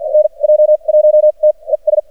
Morse code, heard as a keyed tone, about 48 wpm, spaced as for 36 wpm, 610 Hz, with radio noise. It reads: NJ1T ES
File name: archived / CRSnet